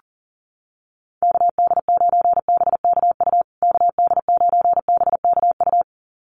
Morse code, heard as a keyed tone, about 40 wpm, 700 Hz. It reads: KD9BKU KD9BKU
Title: USA Call Signs Repeated 2x 40wpm